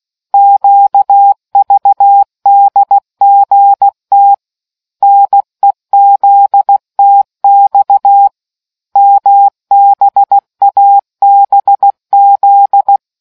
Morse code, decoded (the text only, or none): QVDGT NEZTX MBABZ